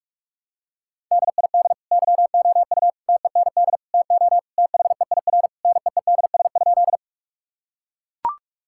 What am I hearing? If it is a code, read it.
Morse code, heard as a keyed tone, about 42 wpm, 700 Hz. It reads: DID YOU TEND TO THEIR NEEDS?
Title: Sentences from Top 1000 Words - Review Part 5 - 45wpm